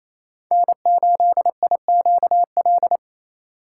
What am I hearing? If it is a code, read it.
Morse code, heard as a keyed tone, about 28 wpm, 700 Hz. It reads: N8IQL
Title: USA Call Signs - 5 Characters - 2x Character Spacing - 28wpm